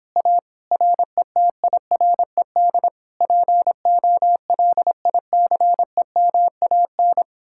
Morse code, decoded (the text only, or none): A RETIRED POLICEMAN